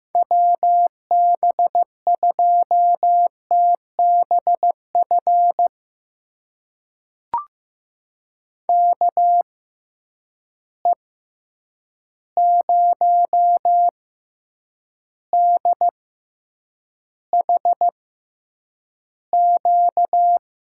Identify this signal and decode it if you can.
Morse code, heard as a keyed tone, about 15 wpm, spaced as for 8 wpm, 700 Hz. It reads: WB2TBF KE0DHQ
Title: USA Call Signs - 6x Character Spacing - 15wpm